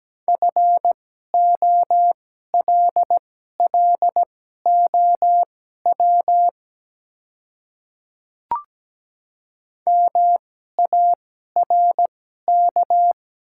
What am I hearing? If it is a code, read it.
Morse code, heard as a keyed tone, about 17 wpm, spaced as for 12 wpm, 700 Hz. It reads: FOLLOW MARK
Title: 2x Character Spacing - Top 300 Words - 17wpm